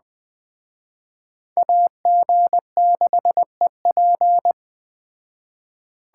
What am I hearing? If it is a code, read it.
Morse code, heard as a keyed tone, about 20 wpm, 700 Hz. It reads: AG6EP
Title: Speed Racing - USA 2x2 Call Signs 17wpm